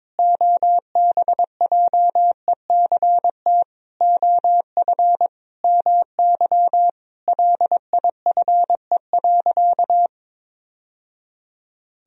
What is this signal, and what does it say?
Morse code, heard as a keyed tone, about 22 wpm, 700 Hz. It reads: OBJECT OF MY LIFE.
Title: Sentences from Top 1000 Words - Review Part 3 - 22wpm